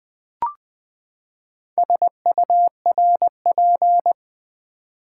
Morse code, heard as a keyed tone, about 20 wpm, 690 Hz. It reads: SURP
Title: Build A Word - 4 Letters At A Time - 10-Letter Words - 20wpm